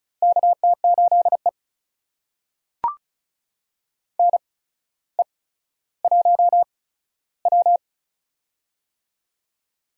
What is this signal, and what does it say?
Morse code, heard as a keyed tone, about 36 wpm, spaced as for 15 wpm, 700 Hz. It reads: KT8E NE1W
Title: USA Call Signs - 4 Characters - 8x Character Spacing - 35wpm